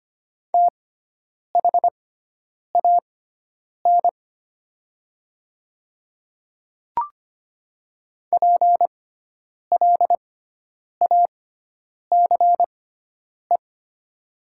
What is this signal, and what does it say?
Morse code, heard as a keyed tone, about 25 wpm, spaced as for 9 wpm, 700 Hz. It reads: THAN PLACE
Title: 6x Character Spacing - Top 200 Words - 25wpm